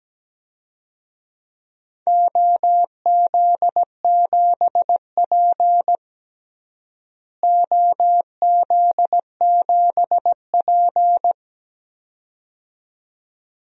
Morse code, heard as a keyed tone, about 17 wpm, 700 Hz. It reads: OZ7P OZ7P
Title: DXCC-100 Call Signs - 4 Characters - Repeated 2x - 17wpm